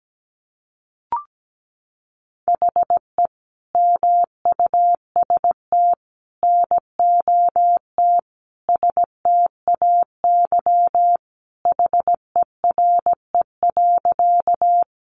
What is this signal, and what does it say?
Morse code, heard as a keyed tone, about 17 wpm, 700 Hz. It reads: HE MUST NOT STAY HERE.
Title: Sentences from Top 600 Words 17wpm